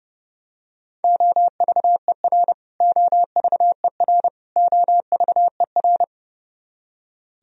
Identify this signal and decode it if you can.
Morse code, heard as a keyed tone, about 32 wpm, 700 Hz. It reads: OVER OVER OVER
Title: Top 100 Words Repeated 30wpm